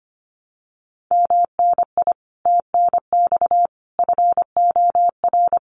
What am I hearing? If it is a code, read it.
Morse code, heard as a keyed tone, about 25 wpm, 700 Hz. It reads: MNI TNX FOR
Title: Common QSO Phrases 25wpm